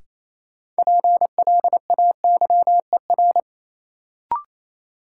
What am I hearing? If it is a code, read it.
Morse code, heard as a keyed tone, about 28 wpm, 700 Hz. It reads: PLAYER
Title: Sentences from Top 1000 Words - Review Part 4 - New Word Drill - 28wpm